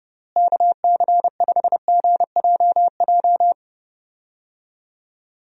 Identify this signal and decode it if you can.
Morse code, heard as a keyed tone, about 32 wpm, 700 Hz. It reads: KC5GJJ
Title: USA Call Signs - 6 Characters - Repeated 2x - 30wpm